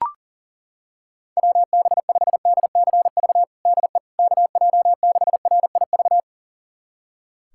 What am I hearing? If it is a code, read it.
Morse code, heard as a keyed tone, about 37 wpm, 700 Hz. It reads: WB5DCV DE KJ6RIV
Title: USA Callsign DE Callsign - 6 Characters - 40wpm